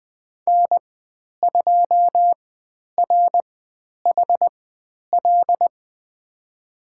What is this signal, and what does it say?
Morse code, heard as a keyed tone, about 20 wpm, spaced as for 10 wpm, 700 Hz. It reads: N2RHL
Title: 095 Lesson 16g - Letter c - Call Signs 20wpm